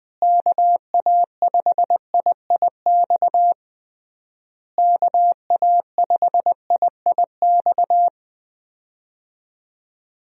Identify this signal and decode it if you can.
Morse code, heard as a keyed tone, about 19 wpm, 700 Hz. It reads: KA5IIX KA5IIX
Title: USA Call Signs - 6 Characters - Repeated 3x - 20wpm